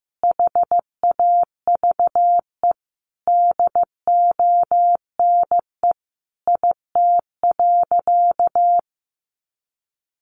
Morse code, heard as a keyed tone, about 15 wpm, 700 Hz. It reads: HAVE DONE IT.
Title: Sentences from Top 1100 Words 15wpm